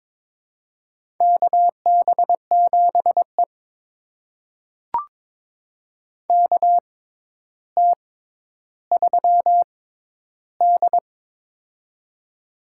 Morse code, heard as a keyed tone, about 22 wpm, spaced as for 11 wpm, 700 Hz. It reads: KB7E KT3D
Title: USA Call Signs - 6x Character Spacing - 22wpm